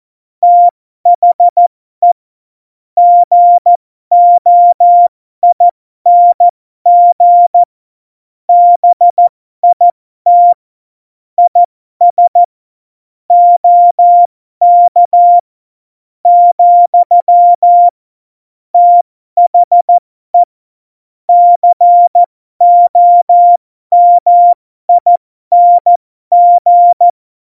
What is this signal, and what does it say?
Morse code, heard as a keyed tone, about 14 wpm, spaced as for 12 wpm, 700 Hz. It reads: THE GOING BIT IS OK , THE COMING
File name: Morsekurs / Witze